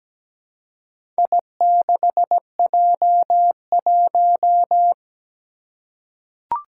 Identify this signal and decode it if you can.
Morse code, heard as a keyed tone, about 17 wpm, 700 Hz. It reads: I6J1
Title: Four Challenging Letter-Number Combinations 17wpm